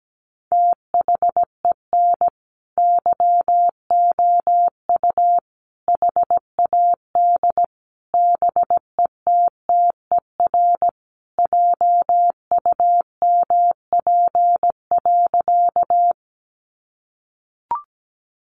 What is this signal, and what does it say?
Morse code, heard as a keyed tone, about 17 wpm, 700 Hz. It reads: THEN YOU HAD BETTER JUMP.
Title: Sentences from Top 1300 Words 17wpm